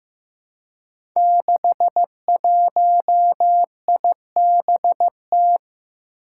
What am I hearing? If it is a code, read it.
Morse code, heard as a keyed tone, about 15 wpm, 700 Hz. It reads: 61IBT
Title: Five Challenging Letter-Number Combinations 15wpm